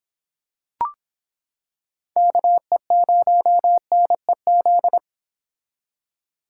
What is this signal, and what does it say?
Morse code, heard as a keyed tone, about 25 wpm, 700 Hz. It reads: KE0NEZ